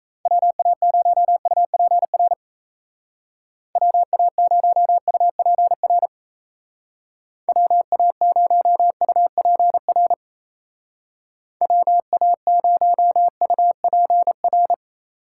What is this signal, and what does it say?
Morse code, heard as a keyed tone, about 37 wpm, 700 Hz. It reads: WA0UPR WA0UPR WA0UPR WA0UPR